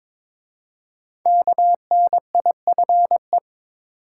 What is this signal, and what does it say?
Morse code, heard as a keyed tone, about 22 wpm, 690 Hz. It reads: KNIFE
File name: 5 Letter Words - First 3 Letters Given - 22wpm